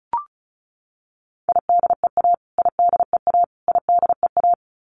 Morse code, heard as a keyed tone, about 34 wpm, 700 Hz. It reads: IDEA IDEA IDEA
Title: Top 300 Words Repeated 35wpm